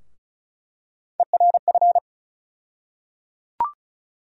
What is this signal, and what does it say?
Morse code, heard as a keyed tone, about 34 wpm, 700 Hz. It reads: ERF
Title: USA Call Sign Postfixes - 3 Letters 35wpm